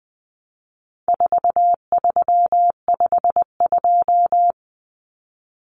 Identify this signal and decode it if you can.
Morse code, heard as a keyed tone, about 20 wpm, 700 Hz. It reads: 4352
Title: Common Cut Numbers - 4-Digit 20wpm